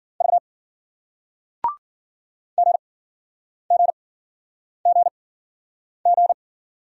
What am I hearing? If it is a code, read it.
Morse code, heard as a keyed tone, about 56 wpm, spaced as for 9 wpm, 700 Hz. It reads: V GGGG